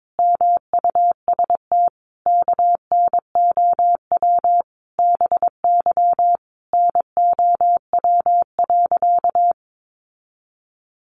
Morse code, heard as a keyed tone, about 22 wpm, 700 Hz. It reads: MUST KNOW BY NOW.
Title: Sentences from Top 1400 Words 22wpm